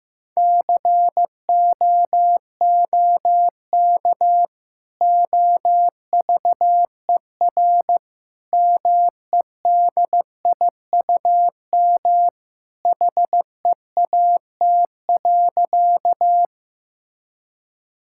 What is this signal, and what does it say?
Morse code, heard as a keyed tone, about 16 wpm, 700 Hz. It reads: COOK OVER MEDIUM HEAT.
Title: Sentences from Top 1000 Words - Review Part 6 - 15wpm